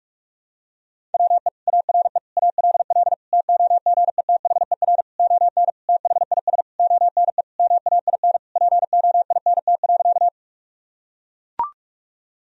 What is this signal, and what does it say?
Morse code, heard as a keyed tone, about 41 wpm, 700 Hz. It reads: WE ARE ALL TOGETHER ON THIS ONE MAIN POINT.
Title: Sentences from Top 1000 Words - Review Part 2 - 45wpm